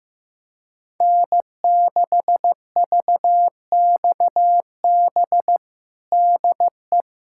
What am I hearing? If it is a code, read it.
Morse code, heard as a keyed tone, about 15 wpm, 700 Hz. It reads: N6VXB DE
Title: USA Callsign DE Callsign - 5 Characters - 15wpm